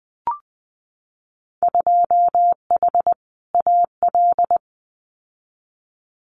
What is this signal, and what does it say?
Morse code, heard as a keyed tone, about 19 wpm, 700 Hz. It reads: 2H AL
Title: Winter Field Day Exchanges 20wpm